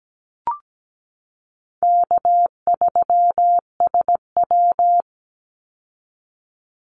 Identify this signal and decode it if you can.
Morse code, heard as a keyed tone, about 17 wpm, 700 Hz. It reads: K3SW
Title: Call Signs Repeated 3x - 17wpm